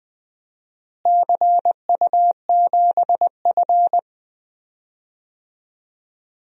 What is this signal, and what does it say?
Morse code, heard as a keyed tone, about 20 wpm, 700 Hz. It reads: CU7F
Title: DXCC-100 Call Signs - 4 Characters - Repeated 2x - 20wpm